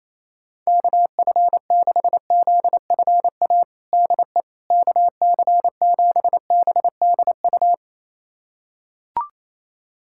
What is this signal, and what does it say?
Morse code, heard as a keyed tone, about 28 wpm, 700 Hz. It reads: KF6ZFA DE KC7BDU
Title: USA Callsign DE Callsign - 6 Characters - 28wpm